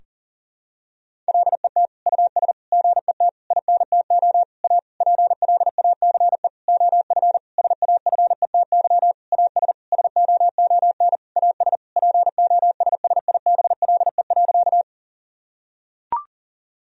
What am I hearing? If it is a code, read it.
Morse code, heard as a keyed tone, about 40 wpm, 700 Hz. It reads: LET US GET INTO A PLACE OF SAFETY AS SOON AS POSSIBLE.
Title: Sentences from Top 1000 Words - Review Part 4 - 40wpm